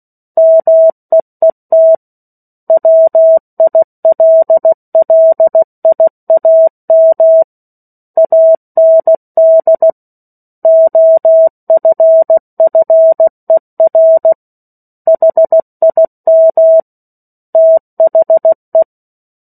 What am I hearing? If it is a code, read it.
Morse code, heard as a keyed tone, about 16 wpm, 650 Hz. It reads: MEET WILLIAM AND OFFER HIM THE
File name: 03-Chapter-03 16 WPM